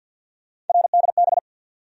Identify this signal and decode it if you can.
Morse code, heard as a keyed tone, about 54 wpm, 700 Hz. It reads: RDB
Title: USA Call Sign Postfixes - 3 Letters 50wpm